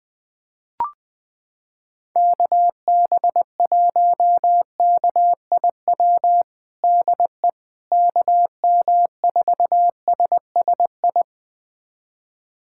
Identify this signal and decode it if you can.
Morse code, heard as a keyed tone, about 20 wpm, 700 Hz. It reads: KB1KIW DE KM4SSI
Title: USA Callsign DE Callsign 20wpm